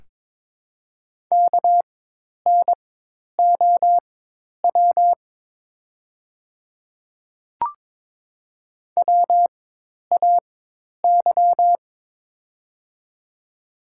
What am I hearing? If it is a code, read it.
Morse code, heard as a keyed tone, about 22 wpm, spaced as for 10 wpm, 700 Hz. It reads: KNOW WAY
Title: 4x Character Spacing - Top 100 Words - 22wpm